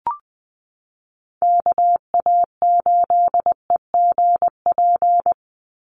Morse code, heard as a keyed tone, about 20 wpm, 700 Hz. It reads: KA8EGP
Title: USA Call Signs 20wpm